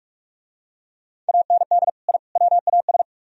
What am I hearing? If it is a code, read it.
Morse code, heard as a keyed tone, about 40 wpm, 700 Hz. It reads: AND I WAS